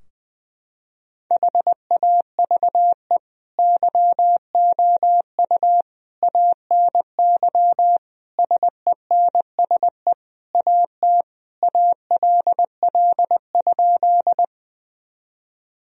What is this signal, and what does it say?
Morse code, heard as a keyed tone, about 21 wpm, 700 Hz. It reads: HAVE YOU ANY SENSE AT ALL?
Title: Sentences from Top 1000 Words - Review Part 4 - 20wpm